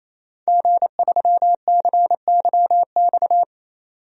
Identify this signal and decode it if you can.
Morse code, heard as a keyed tone, about 28 wpm, 700 Hz. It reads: G3CYX
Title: DXCC-100 Call Signs - 5 Characters - 28wpm